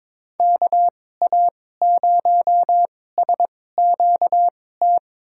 Morse code, heard as a keyed tone, about 22 wpm, spaced as for 16 wpm, 700 Hz. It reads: KA0SQT
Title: USA Call Signs - 6 Characters - 2x Character Spacing - 22wpm